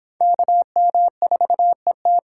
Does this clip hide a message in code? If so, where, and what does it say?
Yes: Morse code from 0.2 to 2.2 s, heard as a keyed tone, about 30 wpm, 700 Hz. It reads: KM4ET